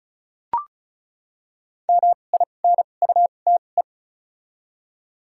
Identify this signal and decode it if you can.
Morse code, heard as a keyed tone, about 36 wpm, spaced as for 25 wpm, 700 Hz. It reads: MINUTE
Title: 2x Character Spacing - Top 500 Words - 35wpm